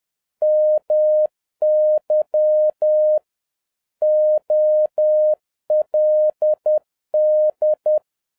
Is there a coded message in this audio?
Yes: Morse code from 0.4 to 8.0 s, heard as a keyed tone, about 10 wpm, 610 Hz. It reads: MY OLD